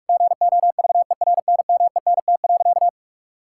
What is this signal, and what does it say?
Morse code, heard as a keyed tone, about 46 wpm, 700 Hz. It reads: GOVERNMENT.
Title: Sentences from Top 1000 Words - Review Part 5 - 45wpm